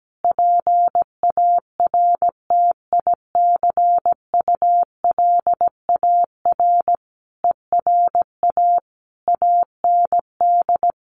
Morse code, heard as a keyed tone, about 17 wpm, 700 Hz. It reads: PARTICULAR ERA AND
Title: A Sound Of Thunder 17wpm